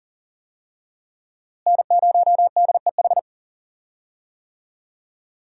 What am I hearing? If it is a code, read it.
Morse code, heard as a keyed tone, about 40 wpm, 700 Hz. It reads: N0DEH
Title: USA Call Signs - 5 Characters - Repeated 3x - 40wpm